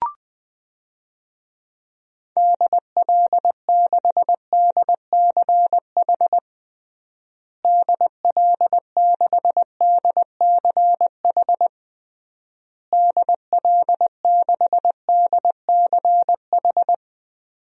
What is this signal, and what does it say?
Morse code, heard as a keyed tone, about 20 wpm, 700 Hz. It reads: DL6DCH DL6DCH DL6DCH
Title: DXCC-100 Call Signs - 6 Characters - Repeated 3x - 20wpm